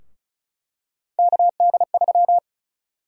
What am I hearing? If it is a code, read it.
Morse code, heard as a keyed tone, about 34 wpm, 700 Hz. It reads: KD3
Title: USA Call Sign Prefixes - 35wpm